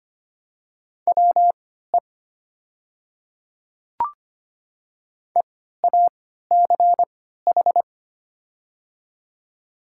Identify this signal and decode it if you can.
Morse code, heard as a keyed tone, about 25 wpm, spaced as for 14 wpm, 700 Hz. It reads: WE EACH
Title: 3x Character Spacing - Top 100 Words - 25wpm